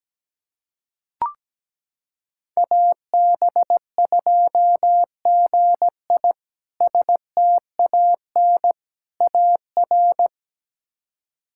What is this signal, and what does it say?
Morse code, heard as a keyed tone, about 17 wpm, 700 Hz. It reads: AB2GI STAN AR
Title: K1USN SST QSO Practice - 17wpm